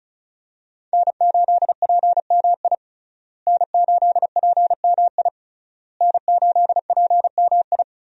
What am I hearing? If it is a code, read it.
Morse code, heard as a keyed tone, about 36 wpm, 700 Hz. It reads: N8PMI N8PMI N8PMI
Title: USA Call Signs - 5 Characters - Repeated 3x - 35wpm